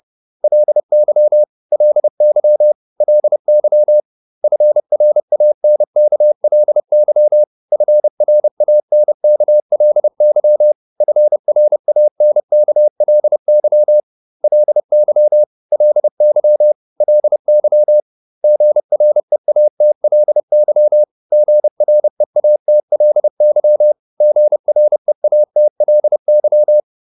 Morse code, heard as a keyed tone, about 30 wpm, 600 Hz. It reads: LY LY LY FRANKLY FRANKLY FRANKLY LY LY LY GREATLY GREATLY GREATLY